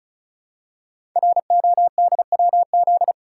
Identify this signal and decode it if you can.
Morse code, heard as a keyed tone, about 36 wpm, 700 Hz. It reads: RODWZ
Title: Code Groups - 5 Letters - 35wpm